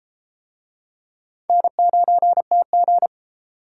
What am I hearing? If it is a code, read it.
Morse code, heard as a keyed tone, about 32 wpm, 700 Hz. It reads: N9TG